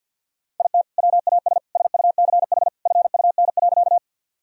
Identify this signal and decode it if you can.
Morse code, heard as a keyed tone, about 50 wpm, 700 Hz. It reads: IT WAS SUCH FUN.
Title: Sentences from Top 1000 Words - Review Part 4 - 50wpm